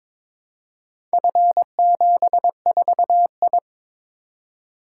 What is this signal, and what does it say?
Morse code, heard as a keyed tone, about 21 wpm, 700 Hz. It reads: F74I
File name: practice / 22wpm / Four Letter-Number Combinations - 22wpm